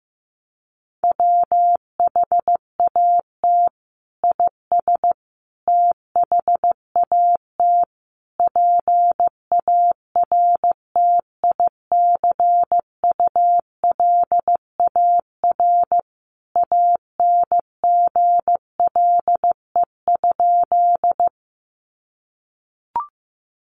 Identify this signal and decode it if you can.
Morse code, heard as a keyed tone, about 15 wpm, 700 Hz. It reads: WHAT IS THAT PARTICULAR ANGLE?
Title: Sentences from Top 1500 Words 15wpm